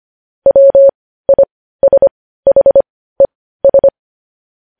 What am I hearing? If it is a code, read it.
Morse code, heard as a keyed tone, about 25 wpm, spaced as for 15 wpm, 550 Hz. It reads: WISHES
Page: news-headlines-in-morse-code-at-15-wpm